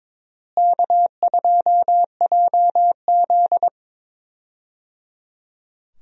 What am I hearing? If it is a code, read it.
Morse code, heard as a keyed tone, about 22 wpm, 700 Hz. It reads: K2JZ
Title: USA Call Signs - 4 Characters - Repeated 3x - 22wpm